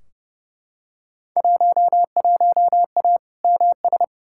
Morse code, heard as a keyed tone, about 31 wpm, 700 Hz. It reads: 11A MS
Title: Field Day Exchanges Encore - 30wpm